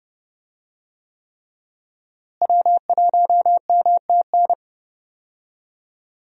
Morse code, heard as a keyed tone, about 30 wpm, 700 Hz. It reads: W1MTN
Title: USA Call Signs - 5 Characters - Repeated 3x - 30wpm